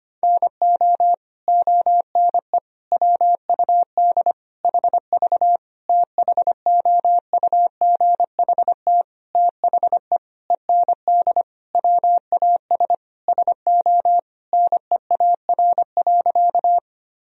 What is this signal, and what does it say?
Morse code, heard as a keyed tone, about 25 wpm, 700 Hz. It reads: NO ONE WUD HV THOUGHT THE END WAS SO NEAR.
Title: Sentences from Top 300 Words and Abbreviations 25wpm